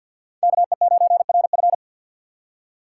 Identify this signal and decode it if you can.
Morse code, heard as a keyed tone, about 54 wpm, 700 Hz. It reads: KE9RF
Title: USA Call Signs - 5 Characters - 6x Character Spacing - 50wpm